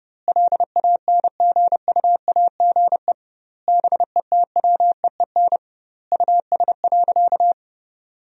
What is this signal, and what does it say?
Morse code, heard as a keyed tone, about 30 wpm, 700 Hz. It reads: LANGUAGE BETWEEN US.
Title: 2x Word-Spacing - Sentences from Top 500 Words - 30wpm